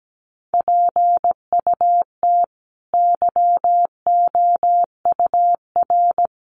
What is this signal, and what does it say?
Morse code, heard as a keyed tone, about 17 wpm, 700 Hz. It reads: PUT YOUR
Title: A Sound Of Thunder 17wpm